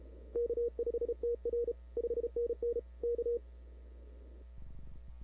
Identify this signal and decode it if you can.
Morse code, heard as a keyed tone, about 30 wpm, 410 Hz. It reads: K5TR 5NN K
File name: CW / 2005-k5tr arrl cw dx -00:00